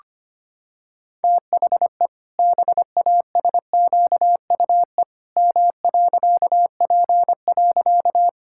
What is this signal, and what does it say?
Morse code, heard as a keyed tone, about 25 wpm, 700 Hz. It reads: THE BASQUE M.P.
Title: Pangram Challenge v2 - 25wpm